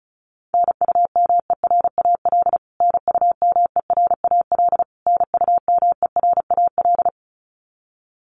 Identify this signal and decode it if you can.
Morse code, heard as a keyed tone, about 34 wpm, 700 Hz. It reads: NUMERAL NUMERAL NUMERAL
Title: Top 500 Words Repeated 35wpm